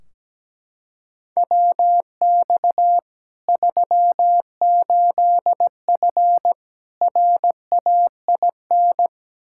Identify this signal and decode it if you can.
Morse code, heard as a keyed tone, about 17 wpm, 700 Hz. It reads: WX 38F RAIN
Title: Practice QSO - Exchange 2 - Info - 17wpm